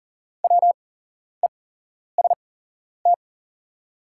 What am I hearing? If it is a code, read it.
Morse code, heard as a keyed tone, about 40 wpm, spaced as for 11 wpm, 700 Hz. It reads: WEST
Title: 8x Character Spacing - Top 500 Words - 40wpm